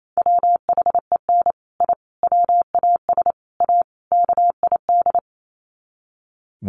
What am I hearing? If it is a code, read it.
Morse code, heard as a keyed tone, about 28 wpm, 700 Hz. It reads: WHEN I WAS A KID
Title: Sets of 5 Words 28wpm